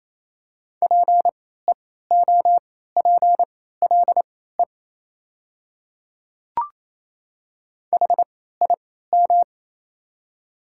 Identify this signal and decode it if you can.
Morse code, heard as a keyed tone, about 28 wpm, spaced as for 16 wpm, 700 Hz. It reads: PEOPLE HIM
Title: 3x Character Spacing - Top 200 Words - 28wpm